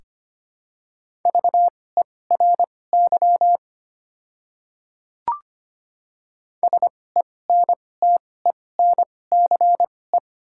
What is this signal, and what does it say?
Morse code, heard as a keyed tone, about 25 wpm, spaced as for 18 wpm, 700 Hz. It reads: VERY SENTENCE
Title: 2x Character Spacing - Top 200 Words - 25wpm